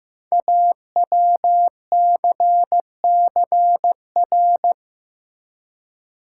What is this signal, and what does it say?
Morse code, heard as a keyed tone, about 15 wpm, 700 Hz. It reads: AWCCR
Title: Code Groups - 5 Letters - 15wpm